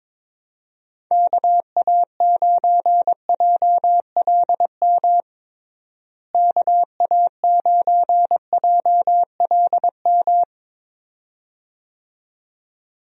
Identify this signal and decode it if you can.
Morse code, heard as a keyed tone, about 22 wpm, 700 Hz. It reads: KA9JLM KA9JLM
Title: USA Call Signs - 6 Characters - Repeated 2x - 22wpm